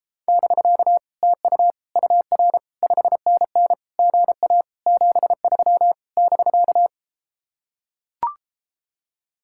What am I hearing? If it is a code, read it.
Morse code, heard as a keyed tone, about 33 wpm, 700 Hz. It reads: <BK> TU UR 5NN GA 73 <BK>